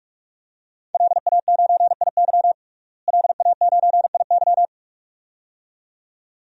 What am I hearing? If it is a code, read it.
Morse code, heard as a keyed tone, about 46 wpm, 700 Hz. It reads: LA9IY LA9IY
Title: DXCC-100 Call Signs - 5 Characters - Repeated 2x - 45wpm